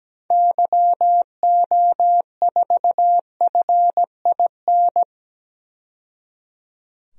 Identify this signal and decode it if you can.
Morse code, heard as a keyed tone, about 17 wpm, 700 Hz. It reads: YO4FIN
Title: DXCC-100 Call Signs - 6 Characters - 17wpm